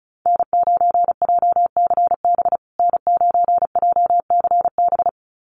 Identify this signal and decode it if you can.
Morse code, heard as a keyed tone, about 34 wpm, 700 Hz. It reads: N9JCB N9JCB
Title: USA Call Signs Repeated 2x 35wpm